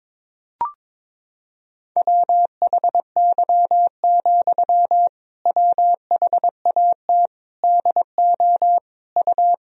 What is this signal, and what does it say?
Morse code, heard as a keyed tone, about 21 wpm, 700 Hz. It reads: WHY, WHAT DO U
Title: Sentences from Top 200 Words and Abbreviations 22wpm